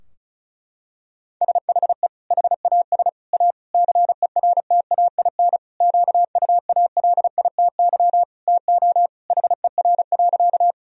Morse code, heard as a keyed tone, about 36 wpm, 700 Hz. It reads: SHE HAS A CERTAIN QUALITY TO HER.
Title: Sentences from Top 1000 Words - Review Part 4 - 35wpm